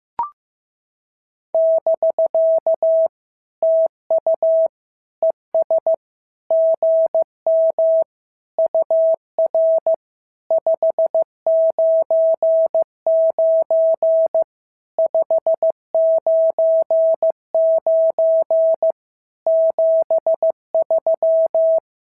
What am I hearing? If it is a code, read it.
Morse code, heard as a keyed tone, about 14 wpm, 650 Hz. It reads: <BK> TU ES GM UR 599 599 73
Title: SOTA QSO Practice 15wpm